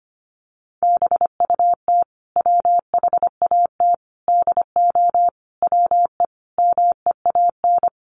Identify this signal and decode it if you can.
Morse code, heard as a keyed tone, about 25 wpm, 700 Hz. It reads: BUT WHAT DO WE MEAN
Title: Sentences from Top 1100 Words 25wpm